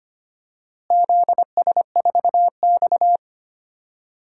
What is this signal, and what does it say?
Morse code, heard as a keyed tone, about 25 wpm, 700 Hz. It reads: ZS4X
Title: DXCC-100 Call Signs - 4 Characters - Repeated 3x - 25wpm